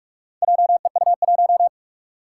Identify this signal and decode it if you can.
Morse code, heard as a keyed tone, about 43 wpm, 700 Hz. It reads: JEU1
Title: Four Challenging Letter-Number Combinations 45wpm